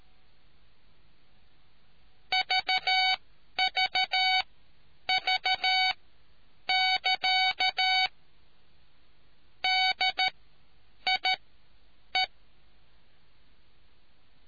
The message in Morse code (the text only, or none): VVV <KA> DIE